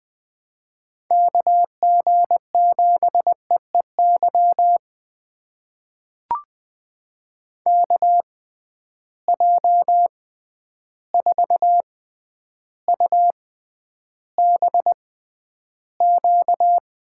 Morse code, heard as a keyed tone, about 20 wpm, spaced as for 10 wpm, 700 Hz. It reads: KG7EEY KJ4UBQ
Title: USA Call Signs - 6 Characters - 6x Character Spacing - 20wpm